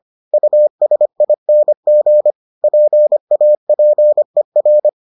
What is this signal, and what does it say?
Morse code, heard as a keyed tone, about 25 wpm, 600 Hz. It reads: USING PAPER